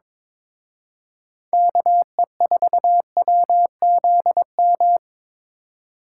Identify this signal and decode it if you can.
Morse code, heard as a keyed tone, about 25 wpm, 700 Hz. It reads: KE4WZM
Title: Speed Racing - USA Call Signs 22wpm